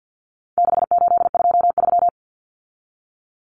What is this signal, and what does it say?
Morse code, heard as a keyed tone, about 50 wpm, 700 Hz. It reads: 6823
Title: Numbers - 4-Digit 50wpm